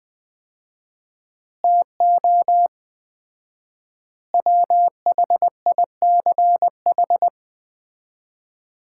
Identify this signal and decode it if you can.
Morse code, heard as a keyed tone, about 21 wpm, 700 Hz. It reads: TO WHICH
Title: 4x Word-Spacing - Sets of 5 Words - 20wpm